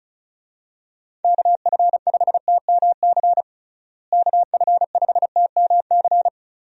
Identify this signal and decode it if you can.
Morse code, heard as a keyed tone, about 34 wpm, 700 Hz. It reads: KF5TMC KF5TMC
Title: USA Call Signs - 6 Characters - Repeated 3x - 35wpm